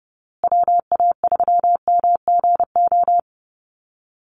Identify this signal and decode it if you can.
Morse code, heard as a keyed tone, about 30 wpm, 700 Hz. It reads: WA3MGO
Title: USA Call Signs Repeated 3x 30wpm